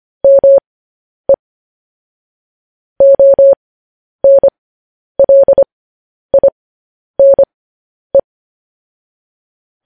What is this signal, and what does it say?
Morse code, heard as a keyed tone, about 25 wpm, spaced as for 10 wpm, 550 Hz. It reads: ME ONLINE
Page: news-headlines-in-morse-code-at-10-wpm